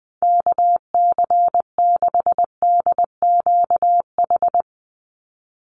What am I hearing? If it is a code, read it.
Morse code, heard as a keyed tone, about 20 wpm, 700 Hz. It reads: KC6DQH
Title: USA Call Signs 20wpm